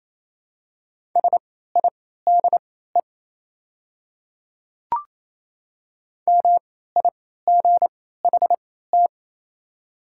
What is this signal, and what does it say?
Morse code, heard as a keyed tone, about 27 wpm, spaced as for 16 wpm, 700 Hz. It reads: SIDE MIGHT
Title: 3x Character Spacing - Top 300 Words - 28wpm